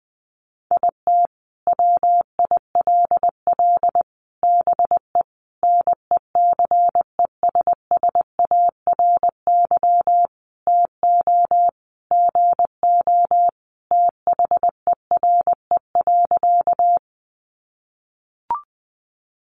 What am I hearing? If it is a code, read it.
Morse code, heard as a keyed tone, about 19 wpm, 700 Hz. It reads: IT WILL BE NECESSARY TO GO THERE.
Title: Sentences from Top 800 Words 20wpm